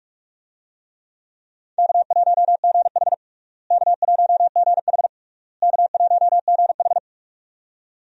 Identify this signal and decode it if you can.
Morse code, heard as a keyed tone, about 46 wpm, 700 Hz. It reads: K1GH K1GH K1GH
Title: USA Call Signs - 4 Characters - Repeated 3x - 45wpm